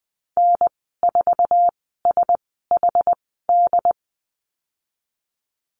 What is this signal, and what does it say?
Morse code, heard as a keyed tone, about 20 wpm, spaced as for 15 wpm, 700 Hz. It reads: N4SHD
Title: 076 Lesson 13g - Letter L - Call Signs 20wpm